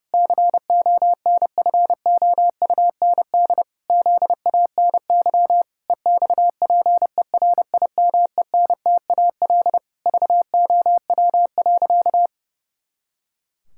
Morse code, heard as a keyed tone, about 30 wpm, 700 Hz. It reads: CONFOUND ZANY EXPERIMENTAL VOW.
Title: Pangram Challenge v2 - 30wpm